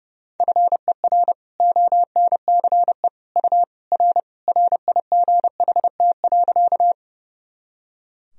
Morse code, heard as a keyed tone, about 30 wpm, 700 Hz. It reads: FER ONCE U R RIGHT.
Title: Sentences from Top 300 Words and Abbreviations 30wpm